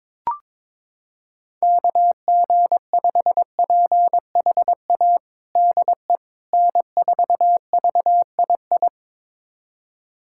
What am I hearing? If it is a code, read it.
Morse code, heard as a keyed tone, about 22 wpm, 700 Hz. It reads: KG5PHA DE N4VII
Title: USA Callsign DE Callsign 22wpm